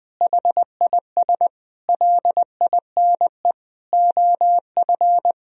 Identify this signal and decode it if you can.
Morse code, heard as a keyed tone, about 20 wpm, 700 Hz. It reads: HIS LINE OF
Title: Sentences from Top 1000 Words - Review Part 4 - 20wpm